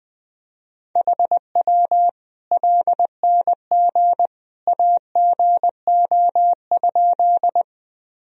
Morse code, heard as a keyed tone, about 21 wpm, 700 Hz. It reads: HW LNG AGO?
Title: Sentences from Top 500 Words and Abbreviations 20wpm